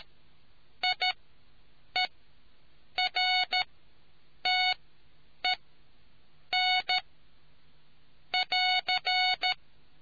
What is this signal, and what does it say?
Morse code, heard as a keyed tone, about 13 wpm, spaced as for 7 wpm, 740 Hz. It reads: IERTEN <AR>